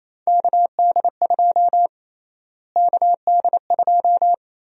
Morse code, heard as a keyed tone, about 28 wpm, 700 Hz. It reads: KD2 KD2
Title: USA Call Sign Prefixes - Repeated 2x - 28wpm